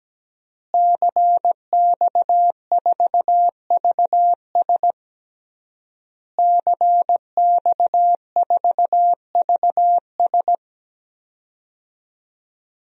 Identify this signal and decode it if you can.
Morse code, heard as a keyed tone, about 17 wpm, 700 Hz. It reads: CX4VS CX4VS
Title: DXCC-100 Call Signs - 5 Characters - Repeated 3x - 17wpm